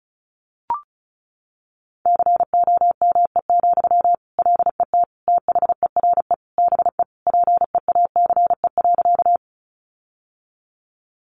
Morse code, heard as a keyed tone, about 34 wpm, 700 Hz. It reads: COME, LET THERE BE PEACE.
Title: Sentences from Top 1500 Words 35wpm